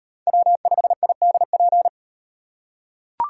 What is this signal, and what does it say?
Morse code, heard as a keyed tone, about 34 wpm, 700 Hz. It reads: W5IDP